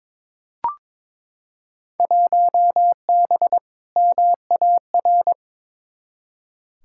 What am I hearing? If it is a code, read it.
Morse code, heard as a keyed tone, about 22 wpm, 700 Hz. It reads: 1B MAR
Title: Field Day Exchanges Encore - 22wpm